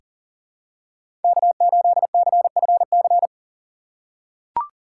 Morse code, heard as a keyed tone, about 40 wpm, 700 Hz. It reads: K8CFC
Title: USA Call Signs - 5 Characters - 6x Character Spacing - 40wpm